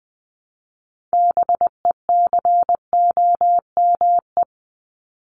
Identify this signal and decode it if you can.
Morse code, heard as a keyed tone, about 20 wpm, 700 Hz. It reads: BECOME
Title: Top 501-750 Words 20wpm